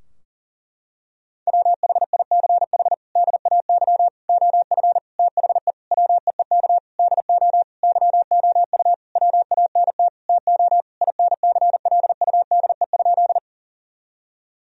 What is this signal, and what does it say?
Morse code, heard as a keyed tone, about 40 wpm, 700 Hz. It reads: WHICH DAY OF THE WEEK DO YOU WANT TO INCLUDE?
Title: Sentences from Top 1000 Words - Review Part 3 - 40wpm